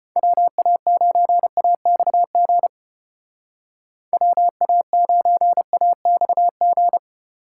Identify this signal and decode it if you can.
Morse code, heard as a keyed tone, about 32 wpm, 700 Hz. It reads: WA9AXG WA9AXG